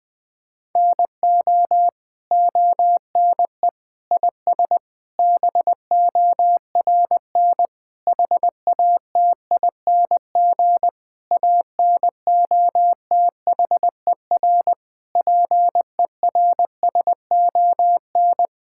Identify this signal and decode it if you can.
Morse code, heard as a keyed tone, about 20 wpm, 700 Hz. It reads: NO ONE IS BORN HATING ANOTHER PERSON
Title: Sets of 9 Words - 20wpm